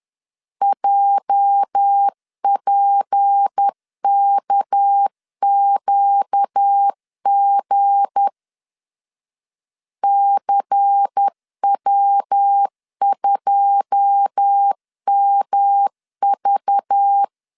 Morse code, heard as a keyed tone, about 10 wpm, 800 Hz. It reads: JPKQG CW2MV
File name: CW / DET / 1 9 1500C S A 10wpm 4511